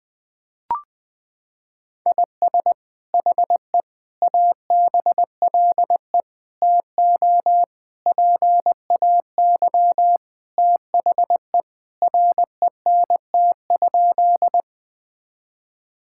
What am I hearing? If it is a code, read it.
Morse code, heard as a keyed tone, about 21 wpm, 700 Hz. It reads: IS HE ABLE TO PAY THE RENT?
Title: Sentences from Top 1000 Words - Review Part 5 - 20wpm